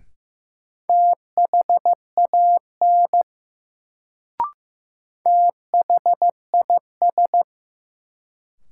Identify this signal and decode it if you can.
Morse code, heard as a keyed tone, about 15 wpm, 700 Hz. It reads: THAN THIS
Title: Sets of 2 Words - New Word Drill - 15wpm